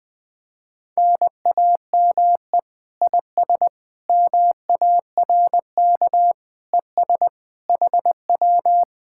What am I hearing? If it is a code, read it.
Morse code, heard as a keyed tone, about 20 wpm, 700 Hz. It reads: NAME IS MARK ES HW
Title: Speed-Racing Common QSO Phrases Encore - 20wpm